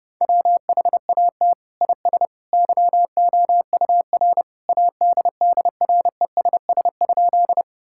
Morse code, heard as a keyed tone, about 32 wpm, 700 Hz. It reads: WHAT IS YOUR ADDRESS?
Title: Sentences from Top 1000 Words - Review Part 6 - 30wpm